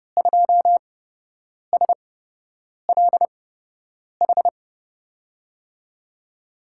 Morse code, heard as a keyed tone, about 29 wpm, spaced as for 8 wpm, 700 Hz. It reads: 2SLH
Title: USA Call Signs - 8x Character Spacing - 30wpm